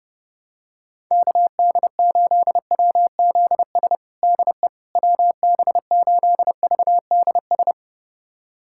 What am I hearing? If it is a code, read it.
Morse code, heard as a keyed tone, about 30 wpm, 700 Hz. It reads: KD8WZS DE WB8VDS
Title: USA Callsign DE Callsign - 6 Characters - 30wpm